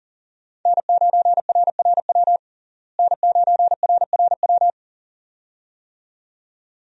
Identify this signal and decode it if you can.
Morse code, heard as a keyed tone, about 37 wpm, 700 Hz. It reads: N9RRW N9RRW
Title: USA Call Signs - 5 Characters - Repeated 2x - 40wpm